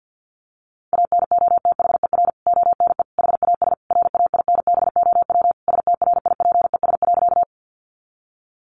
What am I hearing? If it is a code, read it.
Morse code, heard as a keyed tone, about 49 wpm, 700 Hz. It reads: ANOTHER ONE HAS RAINBOW STRIPES.
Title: Additional Easy Phrases 50wpm